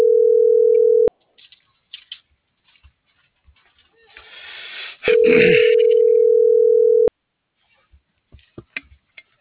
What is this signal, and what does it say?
Ringback tone (North America), 440 + 480 Hz.